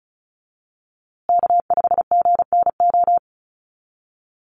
Morse code, heard as a keyed tone, about 34 wpm, 700 Hz. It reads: K5GNO
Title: USA Call Signs Repeated 2x 35wpm